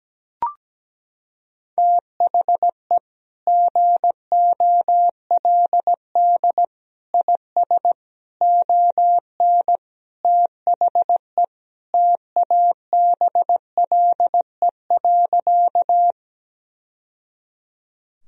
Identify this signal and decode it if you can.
Morse code, heard as a keyed tone, about 17 wpm, 700 Hz. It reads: THE GOLD IS ON THE TABLE.
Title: Sentences from Top 1000 Words - Review Part 3 - 17wpm